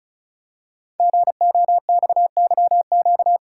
Morse code, heard as a keyed tone, about 36 wpm, 700 Hz. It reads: GOXYQ
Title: Code Groups - 5 Letter:Numbers - 35wpm